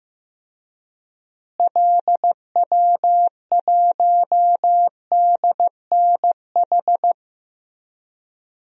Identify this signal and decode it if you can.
Morse code, heard as a keyed tone, about 15 wpm, 700 Hz. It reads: LW1DNH
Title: DXCC-100 Call Signs - 6 Characters - Repeated 2x - 15wpm